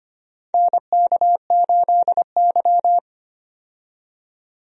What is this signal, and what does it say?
Morse code, heard as a keyed tone, about 25 wpm, 700 Hz. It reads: NK8Y
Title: USA Call Signs - 4 Characters - 25wpm